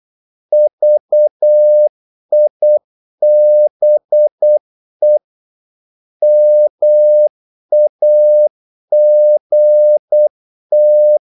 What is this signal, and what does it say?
Morse code, heard as a keyed tone, about 8 wpm, 600 Hz. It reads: VIBE MAGT
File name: Korte ord DK 8wpm